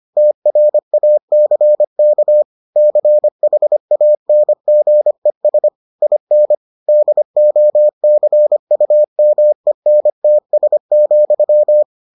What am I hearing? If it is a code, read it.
Morse code, heard as a keyed tone, about 25 wpm, 600 Hz. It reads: TRACK CHANGES IN DOCUMENTS,